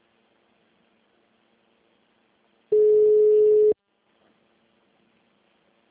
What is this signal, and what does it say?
Ringback tone (Europe, 425 Hz).